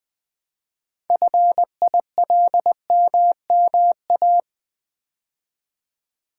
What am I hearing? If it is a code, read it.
Morse code, heard as a keyed tone, about 20 wpm, 700 Hz. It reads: FILMMA
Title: Build A Word - 3 Letters At A Time - 9-Letter Words - 20wpm